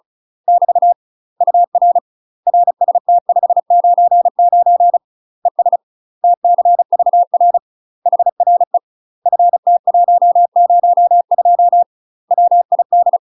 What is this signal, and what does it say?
Morse code, heard as a keyed tone, about 34 wpm, 700 Hz. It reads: <BT> UR RST 599 ES TCVR HRE FT102 WID
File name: UK MASTER QSOs TRAINING / 35WPM / UK MASTER QSOs 0015